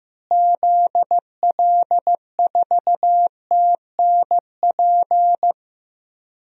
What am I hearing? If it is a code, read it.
Morse code, heard as a keyed tone, about 15 wpm, 700 Hz. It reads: ZL4TNP
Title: DXCC-100 Call Signs - 6 Characters - Repeated 3x - 15wpm